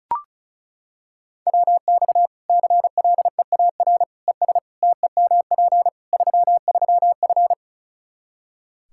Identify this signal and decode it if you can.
Morse code, heard as a keyed tone, about 36 wpm, 700 Hz. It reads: WX CLEAR ES TEMP 33F
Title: Practice QSO - Exchange 2 - Info - 35wpm